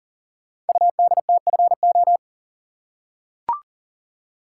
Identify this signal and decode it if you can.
Morse code, heard as a keyed tone, about 40 wpm, 700 Hz. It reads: UDTFO
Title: Code Groups - 5 Letters - 40wpm